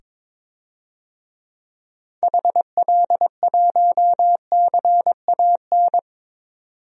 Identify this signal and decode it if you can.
Morse code, heard as a keyed tone, about 22 wpm, 700 Hz. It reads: HL1CAN
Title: DXCC-100 Call Signs - 6 Characters - 22wpm